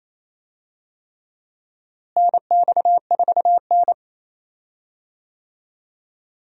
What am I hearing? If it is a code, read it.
Morse code, heard as a keyed tone, about 28 wpm, 700 Hz. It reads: NX4N
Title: USA Call Signs - 4 Characters - Repeated 3x - 28wpm